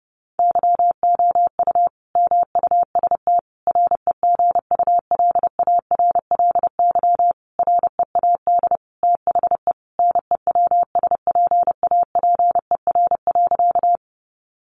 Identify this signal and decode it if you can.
Morse code, heard as a keyed tone, about 30 wpm, 700 Hz. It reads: YOU MUST REGULARLY READ THE NEWSPAPER.
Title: Sentences from Top 1400 Words 30wpm